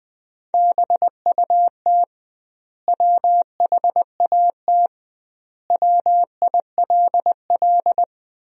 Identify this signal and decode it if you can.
Morse code, heard as a keyed tone, about 20 wpm, 700 Hz. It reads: BUT WHAT WILL
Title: 2x Word-Spacing - Sentences from Top 100 Words - 20wpm